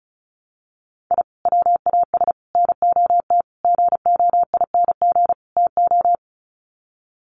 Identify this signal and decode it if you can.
Morse code, heard as a keyed tone, about 34 wpm, 700 Hz. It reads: I WAS NOT GOING TO
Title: Sets of 5 Words 35wpm